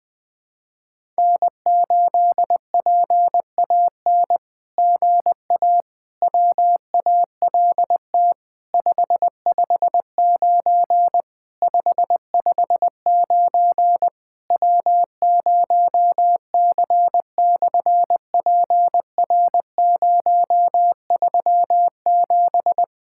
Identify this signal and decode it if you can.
Morse code, heard as a keyed tone, about 20 wpm, 700 Hz. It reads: N8PAN GA WALT 559 559 W0C/PR037
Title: SOTA QSO Practice 20wpm